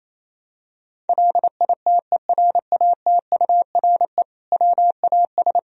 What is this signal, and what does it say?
Morse code, heard as a keyed tone, about 28 wpm, 700 Hz. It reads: LITERATURE WAS
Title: Sentences from Top 1000 Words - Review Part 3 - 28wpm